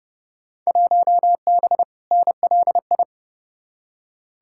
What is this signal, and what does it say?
Morse code, heard as a keyed tone, about 30 wpm, 700 Hz. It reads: 1B NLI
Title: Field Day Exchanges - 30wpm